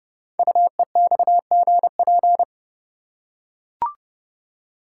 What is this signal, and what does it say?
Morse code, heard as a keyed tone, about 28 wpm, 700 Hz. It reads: UEXGP
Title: Code Groups - 5 Letters - 30wpm